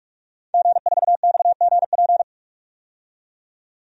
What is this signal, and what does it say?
Morse code, heard as a keyed tone, about 46 wpm, 700 Hz. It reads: G4XGP
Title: DXCC-100 Call Signs - 5 Characters - 45wpm